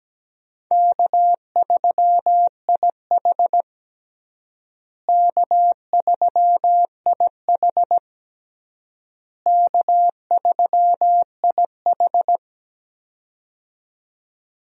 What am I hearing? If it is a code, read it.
Morse code, heard as a keyed tone, about 17 wpm, 700 Hz. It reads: K3IH K3IH K3IH
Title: USA Call Signs - 4 Characters - Repeated 3x - 17wpm